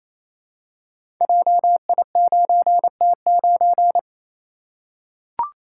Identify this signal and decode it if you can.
Morse code, heard as a keyed tone, about 27 wpm, 690 Hz. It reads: JI9T9
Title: Code Groups - 5 Letter:Numbers - 28wpm